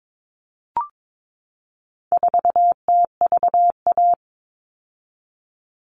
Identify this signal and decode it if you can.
Morse code, heard as a keyed tone, about 22 wpm, 700 Hz. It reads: 4TVA
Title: All Cut Numbers - 4-Digit 22wpm